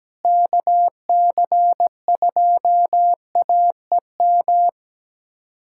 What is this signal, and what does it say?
Morse code, heard as a keyed tone, about 17 wpm, 700 Hz. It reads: KC2AEM
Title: USA Callsign DE Callsign 17wpm